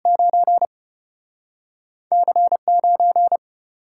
Morse code, heard as a keyed tone, about 34 wpm, 700 Hz. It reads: 9 C9